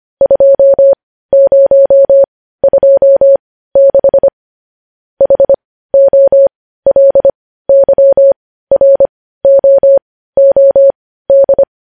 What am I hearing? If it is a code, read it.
Morse code, heard as a keyed tone, about 25 wpm, spaced as for 15 wpm, 550 Hz. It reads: 2026 HOLYROOD
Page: news headlines in morse code at 15 wpm